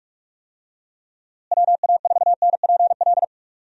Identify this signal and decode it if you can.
Morse code, heard as a keyed tone, about 46 wpm, 700 Hz. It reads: WA4NPL
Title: USA Call Signs - 6 Characters - Repeated 3x - 45wpm